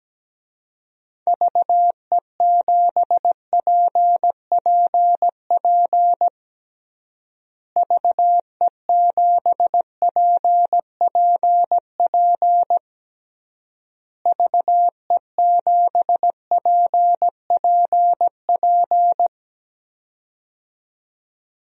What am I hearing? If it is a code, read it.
Morse code, heard as a keyed tone, about 17 wpm, 700 Hz. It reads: VE7PPP VE7PPP VE7PPP
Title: DXCC-100 Call Signs - 6 Characters - Repeated 3x - 17wpm